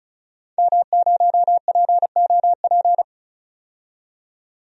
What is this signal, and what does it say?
Morse code, heard as a keyed tone, about 36 wpm, 700 Hz. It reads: M0POP
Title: DXCC-100 Call Signs - 5 Characters - 35wpm